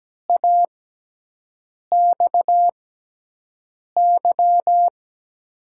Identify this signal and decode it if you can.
Morse code, heard as a keyed tone, about 17 wpm, spaced as for 6 wpm, 700 Hz. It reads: AXY
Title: USA Call Signs - 6 Characters - 6x Character Spacing - 17wpm